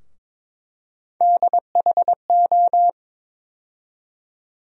Morse code, heard as a keyed tone, about 21 wpm, 700 Hz. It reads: DHO
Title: USA Call Sign Postfixes - 3 Letters 22wpm